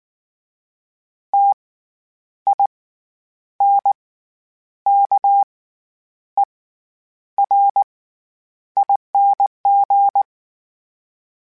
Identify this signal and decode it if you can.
Morse code, heard as a keyed tone, about 19 wpm, spaced as for 9 wpm, 800 Hz. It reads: TINKERING
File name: Special INGs / Special INGs-mp3s / SlowToFaster ing @18